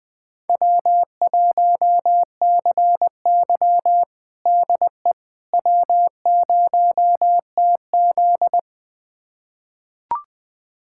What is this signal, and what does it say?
Morse code, heard as a keyed tone, about 20 wpm, 700 Hz. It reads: W1CY DE W0TZ
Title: USA Callsign DE Callsign - 4 Characters - 20wpm